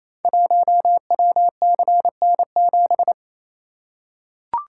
Morse code, heard as a keyed tone, about 28 wpm, 700 Hz. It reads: 1WCN7
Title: Code Groups - 5 Letter:Numbers - 28wpm